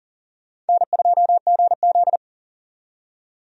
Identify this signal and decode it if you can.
Morse code, heard as a keyed tone, about 40 wpm, 700 Hz. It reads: N2GZ
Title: USA Call Signs - 4 Characters - 40wpm